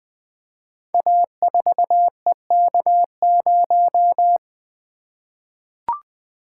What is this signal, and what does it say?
Morse code, heard as a keyed tone, about 20 wpm, 700 Hz. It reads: A4EK0
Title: Code Groups - 5 Letter:Numbers - 20wpm